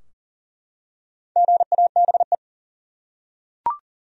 Morse code, heard as a keyed tone, about 40 wpm, 700 Hz. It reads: GABE
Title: USA Common Names - Max 5 Letters - 40wpm